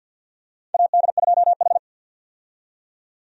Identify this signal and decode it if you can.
Morse code, heard as a keyed tone, about 53 wpm, 700 Hz. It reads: AD2H
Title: USA Call Signs - 4 Characters - 50wpm